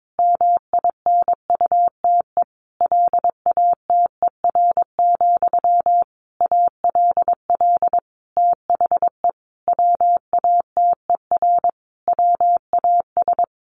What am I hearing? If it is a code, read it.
Morse code, heard as a keyed tone, about 22 wpm, 700 Hz. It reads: MINUTE LATER, ALL THE WATER WAS
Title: Sentences from Top 1100 Words 22wpm